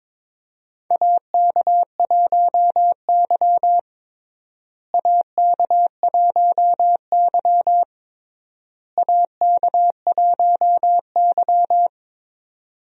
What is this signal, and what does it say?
Morse code, heard as a keyed tone, about 22 wpm, 700 Hz. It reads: AK1Y AK1Y AK1Y
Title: USA Call Signs - 4 Characters - Repeated 3x - 22wpm